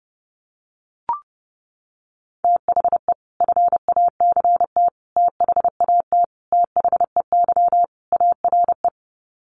Morse code, heard as a keyed tone, about 30 wpm, 700 Hz. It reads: THE FACT THAT THEY ARE
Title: Sets of 5 Words Encore2 30wpm